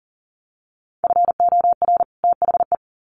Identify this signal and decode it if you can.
Morse code, heard as a keyed tone, about 40 wpm, 700 Hz. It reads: FOR THE
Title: Sets of 2 Words 40wpm